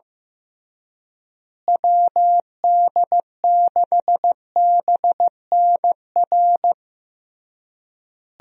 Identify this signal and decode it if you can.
Morse code, heard as a keyed tone, about 15 wpm, 700 Hz. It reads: WD6BNR
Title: USA Call Signs - 6 Characters - Repeated 3x - 15wpm